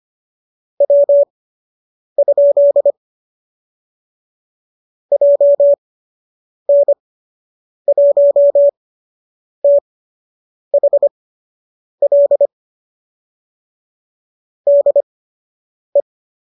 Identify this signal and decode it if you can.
Morse code, heard as a keyed tone, about 25 wpm, spaced as for 8 wpm, 590 Hz. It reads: W? JN1THL DE
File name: qso003 08